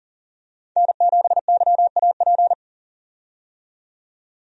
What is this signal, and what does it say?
Morse code, heard as a keyed tone, about 40 wpm, 700 Hz. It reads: N7YAP
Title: USA Call Signs - 5 Characters - 40wpm